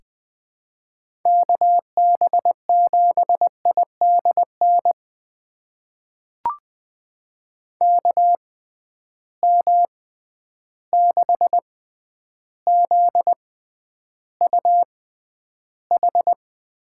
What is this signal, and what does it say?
Morse code, heard as a keyed tone, about 20 wpm, spaced as for 10 wpm, 700 Hz. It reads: KB7IDN KM6ZUH
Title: USA Call Signs - 6 Characters - 6x Character Spacing - 20wpm